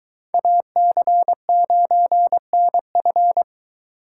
Morse code, heard as a keyed tone, about 22 wpm, 700 Hz. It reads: AC9NF